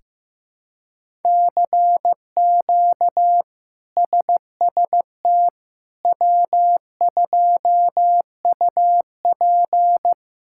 Morse code, heard as a keyed tone, about 15 wpm, 700 Hz. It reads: CQ SST W2UP
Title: K1USN SST QSO Practice - 15wpm